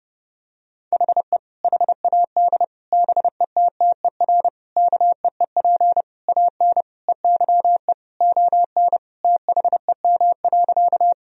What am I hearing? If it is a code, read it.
Morse code, heard as a keyed tone, about 30 wpm, 700 Hz. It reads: HE HAD BETTER KEEP AN EYE ON THEM.
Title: Sentences from Top 1000 Words - Review Part 1 - 30wpm